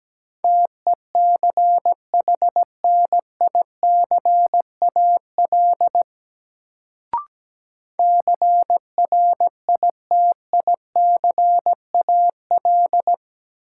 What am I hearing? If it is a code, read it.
Morse code, heard as a keyed tone, about 17 wpm, 700 Hz. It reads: TECHNICAL CRITICAL
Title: Sentences from Top 1100 Words - New Word Drill - 17wpm